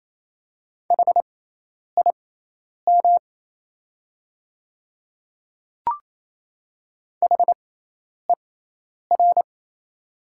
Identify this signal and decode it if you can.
Morse code, heard as a keyed tone, about 28 wpm, spaced as for 10 wpm, 700 Hz. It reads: HIM HER
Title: 6x Character Spacing - Top 200 Words - 28wpm